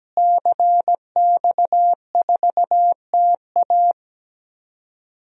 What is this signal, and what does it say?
Morse code, heard as a keyed tone, about 17 wpm, 700 Hz. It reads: CX4TA
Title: DXCC-100 Call Signs - 5 Characters - Repeated 3x - 17wpm